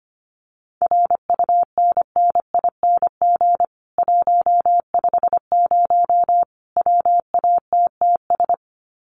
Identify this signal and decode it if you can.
Morse code, heard as a keyed tone, about 25 wpm, 700 Hz. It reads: RUNNING 150 WATTS
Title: Common QSO Phrases 25wpm